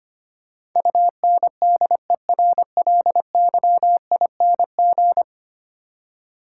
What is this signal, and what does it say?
Morse code, heard as a keyed tone, about 25 wpm, 700 Hz. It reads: UNDERLYING
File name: Ten Letter Words - 25wpm